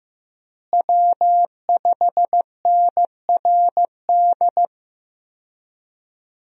Morse code, heard as a keyed tone, about 15 wpm, 700 Hz. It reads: W5NRD
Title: USA Call Signs - 5 Characters - Repeated 2x - 15wpm